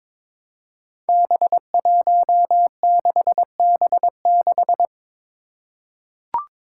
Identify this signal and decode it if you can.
Morse code, heard as a keyed tone, about 22 wpm, 700 Hz. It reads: B16B6
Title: Five Challenging Letter-Number Combinations 22wpm